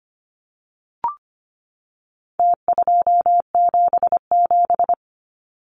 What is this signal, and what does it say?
Morse code, heard as a keyed tone, about 25 wpm, 700 Hz. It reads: T277
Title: Numbers - Contest Style - 4-Digit 25wpm